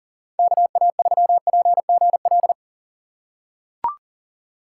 Morse code, heard as a keyed tone, about 37 wpm, 700 Hz. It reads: KA3PGL
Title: USA Call Signs - 6 Characters - 4x Character Spacing - 40wpm